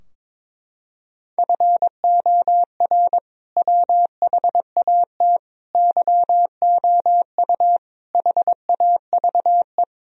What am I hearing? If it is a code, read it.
Morse code, heard as a keyed tone, about 22 wpm, 700 Hz. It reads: FOR WHAT YOU HAVE
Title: Sentences from Top 1000 Words - Review Part 2 - 22wpm